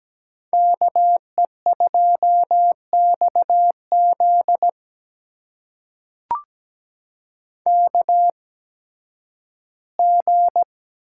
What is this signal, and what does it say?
Morse code, heard as a keyed tone, about 17 wpm, spaced as for 11 wpm, 700 Hz. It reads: KE2XZ KG
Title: USA Call Signs - 5 Characters - 8x Character Spacing - 17wpm